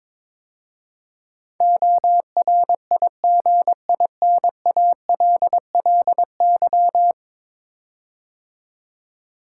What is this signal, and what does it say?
Morse code, heard as a keyed tone, about 22 wpm, 700 Hz. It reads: ORIGINALLY
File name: practice / 22wpm / 10 Letter Words - First 3 Letters Given - 22wpm